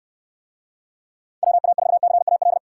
Morse code, heard as a keyed tone, about 68 wpm, 700 Hz. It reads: WA4CAL